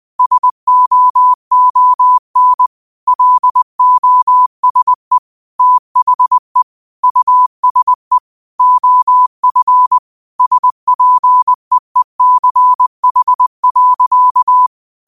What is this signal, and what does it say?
Morse code, heard as a keyed tone, about 20 wpm, 1000 Hz. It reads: SOON LOSE THE USE OF SPEECH.
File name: qotdfiles / 20wpm